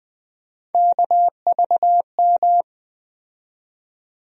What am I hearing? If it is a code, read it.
Morse code, heard as a keyed tone, about 20 wpm, 700 Hz. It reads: KVM
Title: USA Call Sign Postfixes - 3 Letters 20wpm